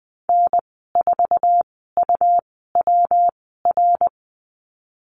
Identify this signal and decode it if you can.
Morse code, heard as a keyed tone, about 20 wpm, spaced as for 14 wpm, 700 Hz. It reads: N4UWR
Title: 147 Lesson 24g - Letter y - Call Signs 20wpm